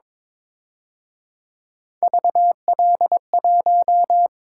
Morse code, heard as a keyed tone, about 26 wpm, 700 Hz. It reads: VL1
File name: Speed-Racing - Three Letter-Number Combinations - 22wpm